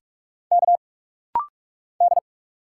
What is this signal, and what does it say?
Morse code, heard as a keyed tone, about 46 wpm, 700 Hz. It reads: K D
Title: Single Letter-Number - Rapid-Fire 45wpm